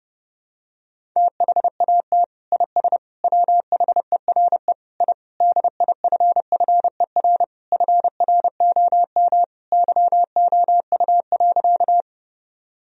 Morse code, heard as a keyed tone, about 31 wpm, 700 Hz. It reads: THAT IS WHERE I DIFFER FROM YOU.
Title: Sentences from Top 1000 Words - Review Part 1 - 30wpm